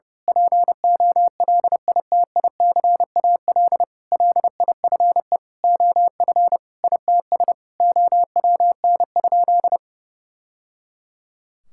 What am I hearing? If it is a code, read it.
Morse code, heard as a keyed tone, about 31 wpm, 700 Hz. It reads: POLITICAL LIFE OF ITS OWN?
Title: Sentences from Top 1000 Words - Review Part 5 - 30wpm